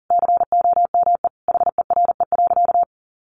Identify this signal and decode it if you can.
Morse code, heard as a keyed tone, about 40 wpm, 700 Hz. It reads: COME HERE.
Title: Sentences from Top 300 Words 40wpm